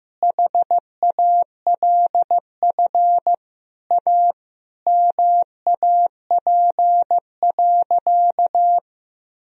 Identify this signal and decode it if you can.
Morse code, heard as a keyed tone, about 15 wpm, 700 Hz. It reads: HALF A MAP.
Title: Sentences from Top 1000 Words - Review Part 2 - 15wpm